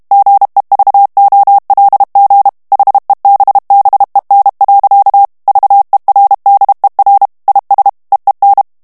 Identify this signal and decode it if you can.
Morse code, heard as a keyed tone, about 32 wpm, 800 Hz. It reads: GEVOLG HEBBEN. VERDER IS EEN
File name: morse mp3 / HOLLANDAIS texte clair 2 vitesse 30 mots minute